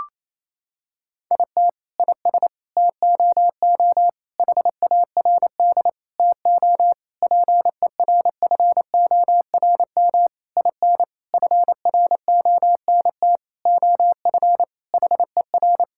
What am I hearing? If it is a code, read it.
Morse code, heard as a keyed tone, about 28 wpm, 700 Hz. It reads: IT IS TOO HARD TO PERFORM IN FRONT OF HER
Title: Sentences from Top 1000 Words - Review Part 5 - 28wpm